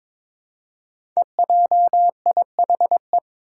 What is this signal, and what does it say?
Morse code, heard as a keyed tone, about 21 wpm, 700 Hz. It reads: EJIHE
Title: Five Challenging Letter-Number Combinations 22wpm